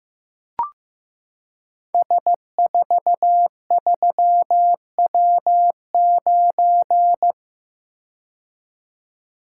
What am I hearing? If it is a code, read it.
Morse code, heard as a keyed tone, about 15 wpm, 700 Hz. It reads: S43W9
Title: Code Groups - 5 Letter:Numbers - 15wpm